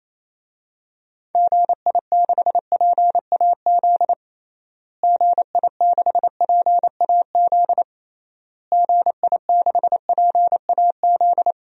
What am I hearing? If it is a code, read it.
Morse code, heard as a keyed tone, about 27 wpm, 700 Hz. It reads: GI6PAZ GI6PAZ GI6PAZ
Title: DXCC-100 Call Signs - 6 Characters - Repeated 3x - 28wpm